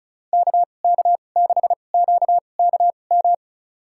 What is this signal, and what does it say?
Morse code, heard as a keyed tone, about 34 wpm, spaced as for 25 wpm, 700 Hz. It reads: KK6QKM
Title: USA Call Signs - 6 Characters - 2x Character Spacing - 35wpm